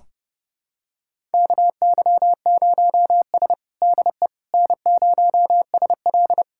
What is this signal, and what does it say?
Morse code, heard as a keyed tone, about 30 wpm, 700 Hz. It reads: KY0S DE N0SL
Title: USA Callsign DE Callsign - 4 Characters - 30wpm